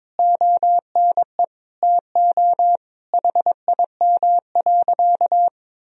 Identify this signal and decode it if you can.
Morse code, heard as a keyed tone, about 22 wpm, 700 Hz. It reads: ONE TO HIM.
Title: Sentences from Top 1000 Words - Review Part 4 - 22wpm